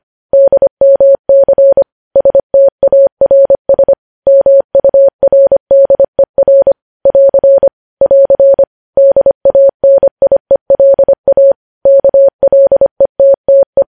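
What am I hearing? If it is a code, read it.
Morse code, heard as a keyed tone, about 25 wpm, 550 Hz. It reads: DMC STARS MURDER <AR> <AR> DANIELA KLETTE
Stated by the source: News headlines sent in Morse code at 25 wpm.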